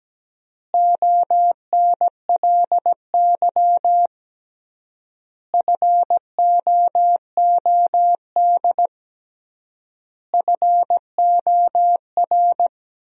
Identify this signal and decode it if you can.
Morse code, heard as a keyed tone, about 17 wpm, 700 Hz. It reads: ONLY FOOD FOR
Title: 3x Word-Spacing - Sentences from Top 300 Words - 17wpm